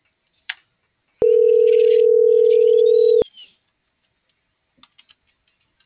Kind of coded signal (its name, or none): telephone call progress tone